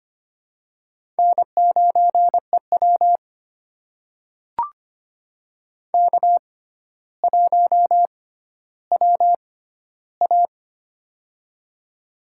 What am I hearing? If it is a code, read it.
Morse code, heard as a keyed tone, about 25 wpm, spaced as for 13 wpm, 700 Hz. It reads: N9EW K1WA
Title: USA Call Signs - 4 Characters - 6x Character Spacing - 25wpm